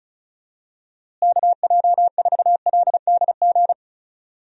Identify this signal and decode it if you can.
Morse code, heard as a keyed tone, about 36 wpm, 700 Hz. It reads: KJ4LDG
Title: USA Call Signs - 6 Characters - Repeated 3x - 35wpm